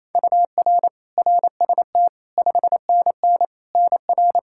Morse code, heard as a keyed tone, about 27 wpm, 700 Hz. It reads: UR RST 5NN NR
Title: K1USN SST QSO Practice 28wpm